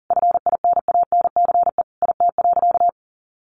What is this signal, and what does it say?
Morse code, heard as a keyed tone, about 40 wpm, 700 Hz. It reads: FINANCE IT.
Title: Sentences from Top 1500 Words 40wpm